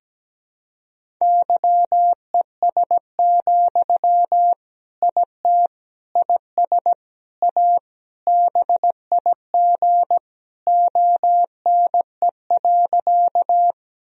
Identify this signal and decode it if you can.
Morse code, heard as a keyed tone, about 17 wpm, 700 Hz. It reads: YES, IT IS A BIG ONE.
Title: Sentences from Top 1000 Words - Review Part 1 - 17wpm